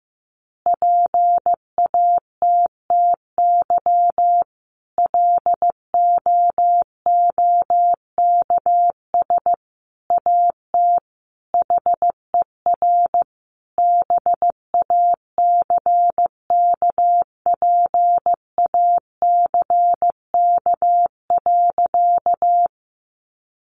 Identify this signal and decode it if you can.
Morse code, heard as a keyed tone, about 15 wpm, 700 Hz. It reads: PATTY LOOKS AT HER BACKPACK.
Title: Additional Easy Phrases 15wpm